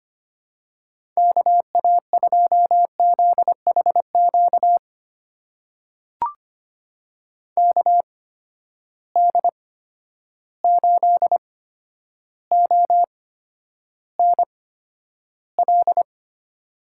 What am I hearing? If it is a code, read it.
Morse code, heard as a keyed tone, about 25 wpm, spaced as for 11 wpm, 700 Hz. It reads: KA2ZHQ KD8ONL
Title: USA Call Signs - 6 Characters - 8x Character Spacing - 25wpm